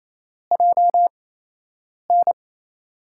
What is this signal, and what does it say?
Morse code, heard as a keyed tone, about 28 wpm, spaced as for 8 wpm, 700 Hz. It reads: JN
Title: USA Call Signs - 6 Characters - 8x Character Spacing - 28wpm